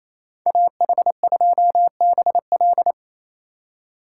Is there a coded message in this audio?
Yes: Morse code from 0.5 to 2.9 s, heard as a keyed tone, about 37 wpm, 700 Hz. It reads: AH2BL